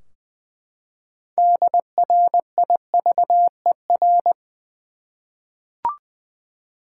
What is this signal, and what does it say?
Morse code, heard as a keyed tone, about 20 wpm, 700 Hz. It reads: DRIVER
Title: Practice QSO - Exchange 3 - More Info - 20wpm